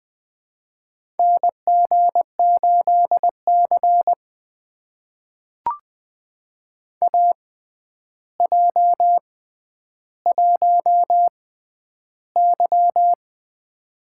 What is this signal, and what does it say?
Morse code, heard as a keyed tone, about 20 wpm, spaced as for 10 wpm, 700 Hz. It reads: NG8C AJ1Y
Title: USA Call Signs - 4 Characters - 6x Character Spacing - 20wpm